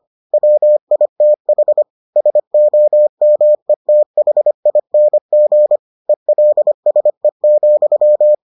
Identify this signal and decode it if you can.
Morse code, heard as a keyed tone, about 25 wpm, 600 Hz. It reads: WITH SOMETHING ELSE,